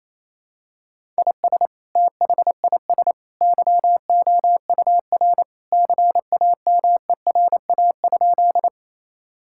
Morse code, heard as a keyed tone, about 28 wpm, 700 Hz. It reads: IS THIS YOUR CAMERA?
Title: Sentences from Top 1000 Words - Review Part 5 - 28wpm